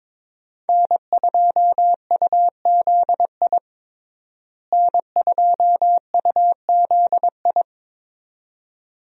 Morse code, heard as a keyed tone, about 22 wpm, 700 Hz. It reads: N2UZI N2UZI
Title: USA Call Signs - 5 Characters - Repeated 2x - 22wpm